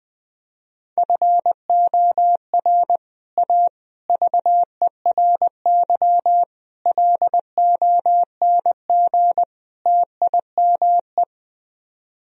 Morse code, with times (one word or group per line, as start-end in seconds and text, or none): 1.0-3.0 FOR
3.4-3.7 A
4.1-6.4 VERY
6.9-9.4 LONG
9.9-11.2 TIME